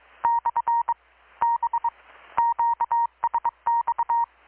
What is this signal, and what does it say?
Morse code, heard as a keyed tone, about 22 wpm, 940 Hz. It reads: / B QSX
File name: blog / swl / 6-abril-2019-12695-USB-KPS